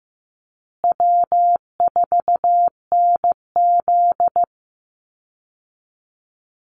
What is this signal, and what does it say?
Morse code, heard as a keyed tone, about 15 wpm, 700 Hz. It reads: W4NZ
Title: Call Signs Repeated 3x - 15wpm